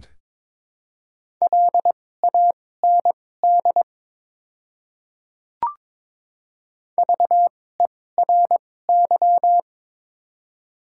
Morse code, heard as a keyed tone, about 22 wpm, spaced as for 16 wpm, 700 Hz. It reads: LAND VERY
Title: 2x Character Spacing - Top 200 Words - 22wpm